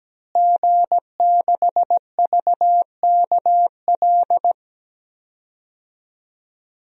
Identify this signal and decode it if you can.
Morse code, heard as a keyed tone, about 17 wpm, 700 Hz. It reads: G6VKL
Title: DXCC-100 Call Signs - 5 Characters - Repeated 3x - 17wpm